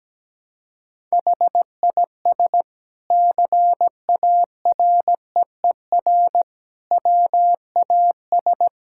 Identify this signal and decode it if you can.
Morse code, heard as a keyed tone, about 17 wpm, 700 Hz. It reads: HIS CAREER WAS
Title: Sentences from Top 1000 Words - Review Part 6 - 17wpm